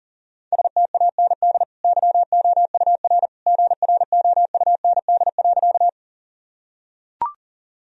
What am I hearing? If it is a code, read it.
Morse code, heard as a keyed tone, about 40 wpm, 700 Hz. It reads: STAND YOUR GROUND.
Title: Sentences from Top 1000 Words - Review Part 2 - 40wpm